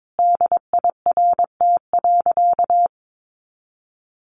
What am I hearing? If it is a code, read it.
Morse code, heard as a keyed tone, about 22 wpm, 700 Hz. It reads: DIRT.
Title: A Sound Of Thunder 22wpm